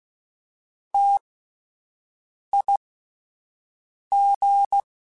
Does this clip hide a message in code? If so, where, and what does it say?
Yes: Morse code from 0.9 to 4.8 s, heard as a keyed tone, about 16 wpm, spaced as for 5 wpm, 800 Hz. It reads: TIG